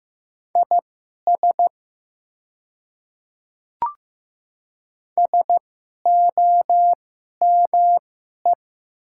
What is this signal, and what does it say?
Morse code, heard as a keyed tone, about 15 wpm, spaced as for 11 wpm, 700 Hz. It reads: IS SOME
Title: 2x Character Spacing - Top 100 Words - 15wpm